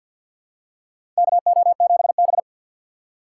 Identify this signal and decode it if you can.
Morse code, heard as a keyed tone, about 51 wpm, 700 Hz. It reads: KO7B
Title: USA Call Signs - 3x Character Spacing - 50wpm